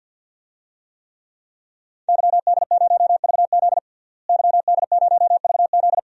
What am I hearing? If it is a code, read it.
Morse code, heard as a keyed tone, about 48 wpm, 700 Hz. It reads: YD0VZ YD0VZ
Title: DXCC-100 Call Signs - 5 Characters - Repeated 2x - 50wpm